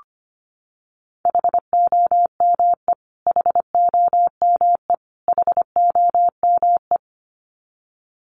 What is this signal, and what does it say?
Morse code, heard as a keyed tone, about 25 wpm, 700 Hz. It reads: HOME HOME HOME
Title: Top 300 Words Repeated 25wpm